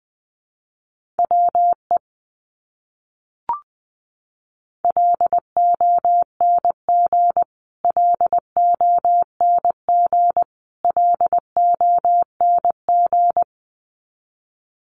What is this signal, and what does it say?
Morse code, heard as a keyed tone, about 20 wpm, 700 Hz. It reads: WE LONG LONG LONG
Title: Top 200 Words Repeated 20wpm